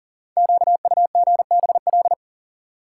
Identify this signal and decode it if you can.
Morse code, heard as a keyed tone, about 42 wpm, 690 Hz. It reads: QUGBL
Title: Code Groups - 5 Letters - 40wpm